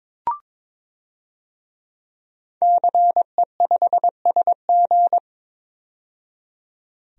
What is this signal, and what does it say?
Morse code, heard as a keyed tone, about 22 wpm, 700 Hz. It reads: CE5SG
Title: DXCC-100 Call Signs - 5 Characters - 22wpm